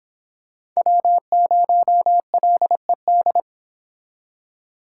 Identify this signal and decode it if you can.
Morse code, heard as a keyed tone, about 26 wpm, 700 Hz. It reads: W0LED